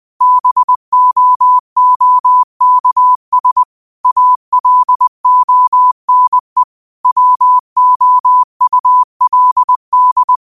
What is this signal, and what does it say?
Morse code, heard as a keyed tone, about 20 wpm, 1000 Hz. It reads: BOOKS ALONE WOULD
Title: 20 WPM morse code quotes for Thu, 29 Feb 2024